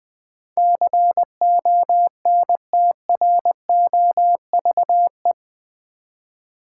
Morse code, heard as a keyed tone, about 20 wpm, 700 Hz. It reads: CONTROVE
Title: Build A Word - 4 Letters At A Time - 11-Letter Words - 20wpm